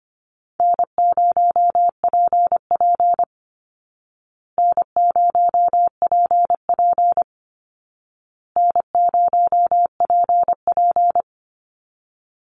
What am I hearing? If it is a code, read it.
Morse code, heard as a keyed tone, about 24 wpm, 700 Hz. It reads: N0PP N0PP N0PP
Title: Call Signs Repeated 2x - 25wpm